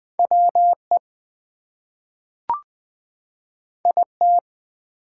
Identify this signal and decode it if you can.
Morse code, heard as a keyed tone, about 20 wpm, 700 Hz. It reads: WE IT
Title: Two Letter Words - 20wpm